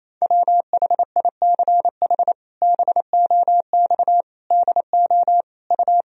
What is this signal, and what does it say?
Morse code, heard as a keyed tone, about 28 wpm, 700 Hz. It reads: WHICH BOX DO U
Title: Sentences from Top 500 Words and Abbreviations 28wpm